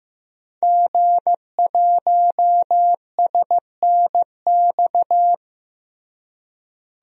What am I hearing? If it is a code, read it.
Morse code, heard as a keyed tone, about 15 wpm, 700 Hz. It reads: G1SNX
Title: DXCC-100 Call Signs - 5 Characters - Repeated 3x - 15wpm